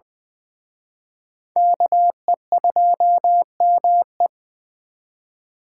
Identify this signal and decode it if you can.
Morse code, heard as a keyed tone, about 24 wpm, 700 Hz. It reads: KE2ME
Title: Speed Racing - USA 2x2 Call Signs 20wpm